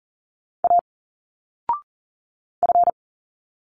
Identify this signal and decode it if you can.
Morse code, heard as a keyed tone, about 40 wpm, 700 Hz. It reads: A F
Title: Single Letters 40wpm